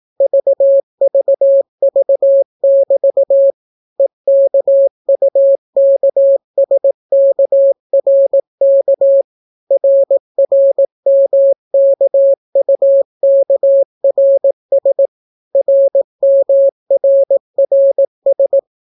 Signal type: Morse code